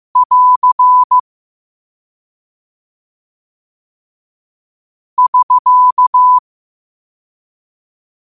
Morse code, heard as a keyed tone, about 15 wpm, 1000 Hz. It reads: <AR> <SK>